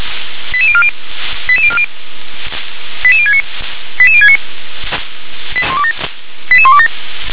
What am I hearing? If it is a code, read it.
Five-tone selective call (ZVEI), heard as quick runs of five tones, tones of about 70 ms. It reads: 88040 88040 88069 88069 88127